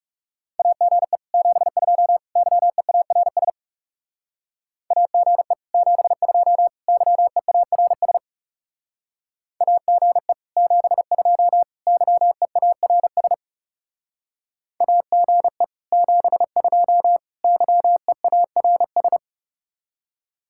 Morse code, heard as a keyed tone, about 37 wpm, 700 Hz. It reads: AGE 72 YEARS AGE 72 YEARS AGE 72 YEARS AGE 72 YEARS
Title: Speed-Racing Common QSO Phrases - 30wpm